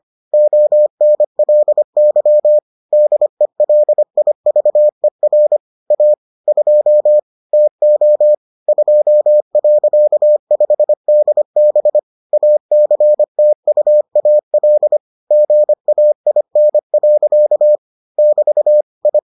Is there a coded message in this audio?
Yes: Morse code from 0.3 to 19.2 s, heard as a keyed tone, about 25 wpm, 600 Hz. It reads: ONLY DELIVER A 2 TO 2.5DB ACTUAL GAIN. <BT> I